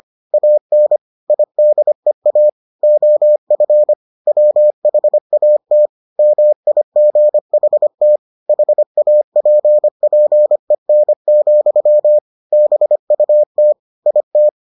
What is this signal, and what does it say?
Morse code, heard as a keyed tone, about 25 wpm, 600 Hz. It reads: AN IDEA OF WHAT MIGHT HAPPEN, BUT IT